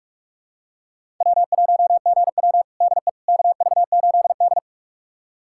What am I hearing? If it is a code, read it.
Morse code, heard as a keyed tone, about 46 wpm, 700 Hz. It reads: W1GW DE KV8D
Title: USA Callsign DE Callsign - 4 Characters - 45wpm